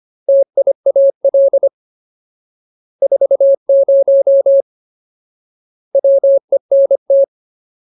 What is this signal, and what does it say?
Morse code, heard as a keyed tone, about 25 wpm, 550 Hz. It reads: TIAL 40 WENT
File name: Ultimate Word List / 25WPM Ultimate